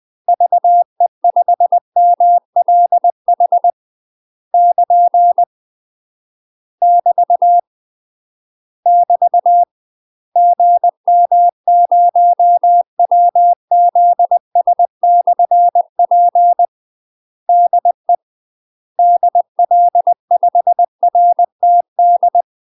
Morse code, heard as a keyed tone, about 20 wpm, 700 Hz. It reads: VE5MLH <KN> <BT> <BT> GM0WZS/P DE DL5RTD
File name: UK MASTER QSOs TRAINING/20WPM